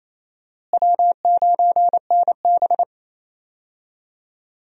Morse code, heard as a keyed tone, about 27 wpm, 700 Hz. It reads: W9NB
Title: USA Call Signs - 4 Characters - Repeated 2x - 28wpm